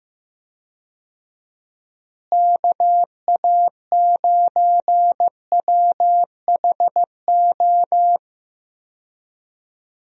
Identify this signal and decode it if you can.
Morse code, heard as a keyed tone, about 15 wpm, 700 Hz. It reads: KA9WHO
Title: USA Call Signs - 6 Characters - Repeated 2x - 15wpm